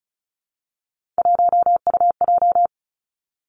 Morse code, heard as a keyed tone, about 36 wpm, 700 Hz. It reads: 1UJ
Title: Three Challenging Letter-Number Combinations 35wpm